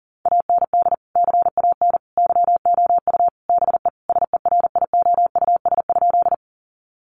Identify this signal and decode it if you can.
Morse code, heard as a keyed tone, about 40 wpm, 700 Hz. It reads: AND CAN YOU BE SERIOUS?
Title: Sentences from Top 1100 Words 40wpm